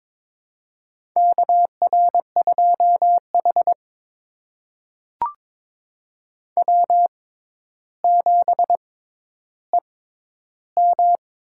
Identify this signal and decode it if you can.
Morse code, heard as a keyed tone, about 22 wpm, spaced as for 11 wpm, 700 Hz. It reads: KR2H W7EM
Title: USA Call Signs - 4 Characters - 6x Character Spacing - 22wpm